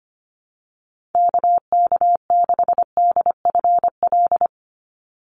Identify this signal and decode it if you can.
Morse code, heard as a keyed tone, about 25 wpm, 700 Hz. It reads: KK6DFL
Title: USA Call Signs 25wpm